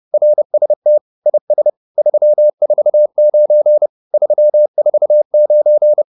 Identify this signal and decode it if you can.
Morse code, heard as a keyed tone, about 30 wpm, 600 Hz. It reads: RST IS 349 349